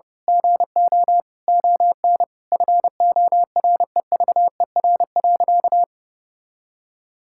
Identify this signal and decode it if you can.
Morse code, heard as a keyed tone, about 32 wpm, 700 Hz. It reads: GO ON FOREVER.
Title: Sentences from Top 1600 Words 30wpm